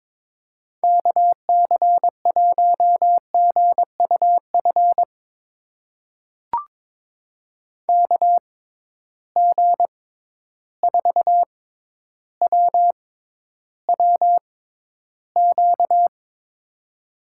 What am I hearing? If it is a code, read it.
Morse code, heard as a keyed tone, about 22 wpm, spaced as for 11 wpm, 700 Hz. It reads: KC1GUF KG4WWQ
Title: USA Call Signs - 6 Characters - 6x Character Spacing - 22wpm